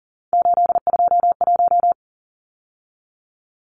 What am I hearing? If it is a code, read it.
Morse code, heard as a keyed tone, about 40 wpm, 700 Hz. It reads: 821
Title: Numbers - 3-Digit 40wpm